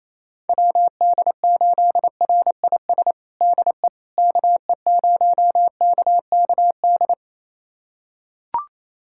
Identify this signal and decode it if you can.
Morse code, heard as a keyed tone, about 28 wpm, 700 Hz. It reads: WD8RIS DE KE0KKD
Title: USA Callsign DE Callsign - 6 Characters - 28wpm